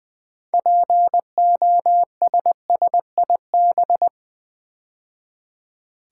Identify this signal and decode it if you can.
Morse code, heard as a keyed tone, about 20 wpm, 700 Hz. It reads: POSSIB
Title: Build A Word - 2 Letters At A Time - 8-Letter Words - 20wpm